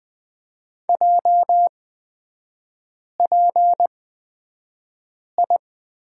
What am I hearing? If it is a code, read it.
Morse code, heard as a keyed tone, about 20 wpm, spaced as for 5 wpm, 700 Hz. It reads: JPI
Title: 236 Lesson 38e - Letter q - Three Characters 20wpm